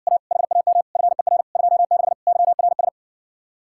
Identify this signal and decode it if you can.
Morse code, heard as a keyed tone, about 70 wpm, 700 Hz. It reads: A HAM FER 26 YRS